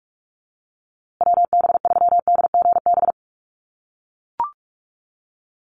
Morse code, heard as a keyed tone, about 44 wpm, 700 Hz. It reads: WB3DGB
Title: USA Call Signs Repeated 3x 45wpm